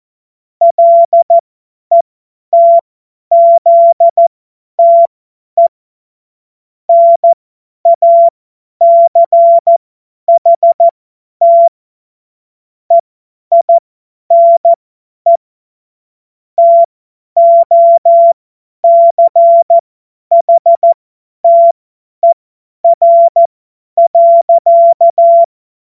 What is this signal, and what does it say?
Morse code, heard as a keyed tone, about 14 wpm, spaced as for 10 wpm, 680 Hz. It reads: LETZTE NACHT EINE TOCHTER.